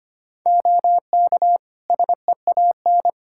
Morse code, heard as a keyed tone, about 25 wpm, 700 Hz. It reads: OK SEAN
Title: Practice QSO - Exchange 3 - More Info - 25wpm